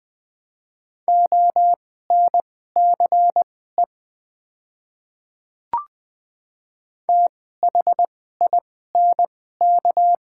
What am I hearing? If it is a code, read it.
Morse code, heard as a keyed tone, about 20 wpm, spaced as for 14 wpm, 690 Hz. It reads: ONCE THINK
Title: 2x Character Spacing - Top 300 Words - 20wpm